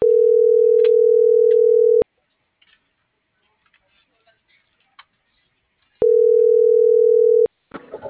Ringback tone (North America), 440 + 480 Hz.